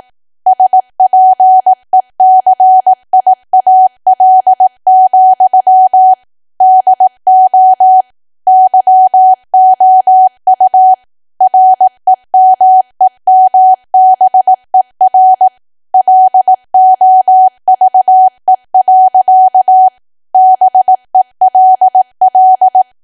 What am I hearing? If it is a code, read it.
Morse code, heard as a keyed tone, about 18 wpm, 750 Hz. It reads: SPECIAL, DO YOU REMEMBER LOVE. BELL
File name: Archive / 18 WPM / 18WPM